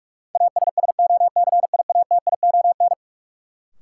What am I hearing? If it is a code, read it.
Morse code, heard as a keyed tone, about 46 wpm, 700 Hz. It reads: ASSOCIATION
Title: Sentences from Top 1100 Words - New Word Drill Alt - 45wpm